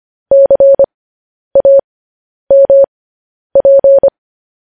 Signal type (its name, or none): Morse code